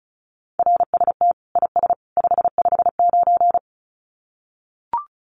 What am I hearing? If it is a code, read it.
Morse code, heard as a keyed tone, about 34 wpm, 700 Hz. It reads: RST IS 559
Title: Common QSO Phrases 35wpm